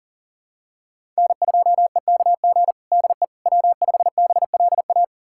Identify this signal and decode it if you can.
Morse code, heard as a keyed tone, about 40 wpm, 700 Hz. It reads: N2EKG DE W5BLA
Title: USA Callsign DE Callsign - 5 Characters - 40wpm